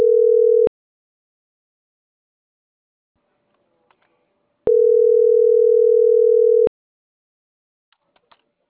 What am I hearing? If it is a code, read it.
Ringback tone (North America), 440 + 480 Hz.